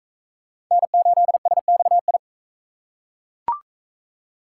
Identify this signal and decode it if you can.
Morse code, heard as a keyed tone, about 40 wpm, 700 Hz. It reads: N8SXI